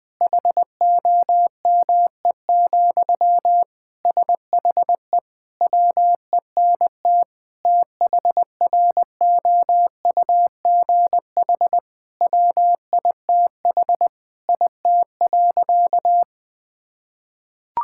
Morse code, heard as a keyed tone, about 20 wpm, 700 Hz. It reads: HOME, SHE WENT THROUGH WITH IT.
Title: Sentences from Top 1000 Words - Review Part 6 - 20wpm